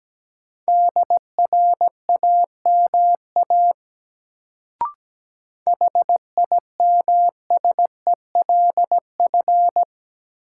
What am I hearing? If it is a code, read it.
Morse code, heard as a keyed tone, about 17 wpm, 700 Hz. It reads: DRAMA HIMSELF
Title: Sentences from Top 1500 Words - New Word Drill - 17wpm